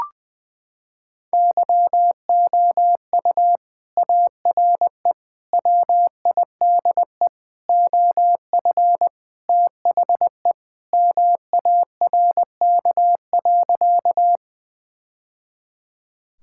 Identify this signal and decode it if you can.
Morse code, heard as a keyed tone, about 21 wpm, 700 Hz. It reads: YOU ARE WIDE OF THE MARK.
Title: Sentences from Top 1000 Words - Review Part 6 - 20wpm